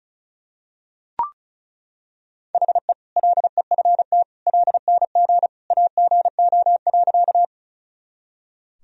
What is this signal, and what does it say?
Morse code, heard as a keyed tone, about 36 wpm, 700 Hz. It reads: HE LEFT LNG AGO.
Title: Sentences from Top 500 Words and Abbreviations 35wpm